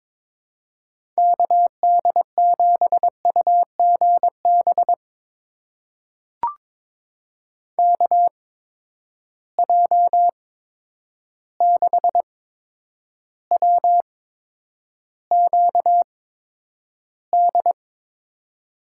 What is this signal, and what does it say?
Morse code, heard as a keyed tone, about 22 wpm, spaced as for 9 wpm, 700 Hz. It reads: KD7UGB KJ6WQD
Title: USA Call Signs - 6 Characters - 8x Character Spacing - 22wpm